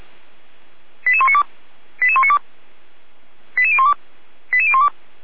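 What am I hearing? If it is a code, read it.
Five-tone selective call (ZVEI), heard as quick runs of five tones, tones of about 70 ms. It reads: 88192 88192 88012 88012
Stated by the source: ZVEI five-tone selective call received on an SDR, from feuerwehr-orlamuende.org (fire brigade Orlamünde).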